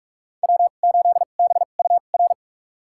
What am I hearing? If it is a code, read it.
Morse code, heard as a keyed tone, about 42 wpm, spaced as for 31 wpm, 700 Hz. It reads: W8BUR
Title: USA Call Signs - 2x Character Spacing - 45wpm